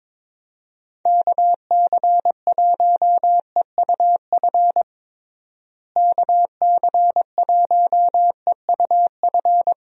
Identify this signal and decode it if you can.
Morse code, heard as a keyed tone, about 21 wpm, 700 Hz. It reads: KC1EUF KC1EUF
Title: USA Call Signs - 6 Characters - Repeated 2x - 22wpm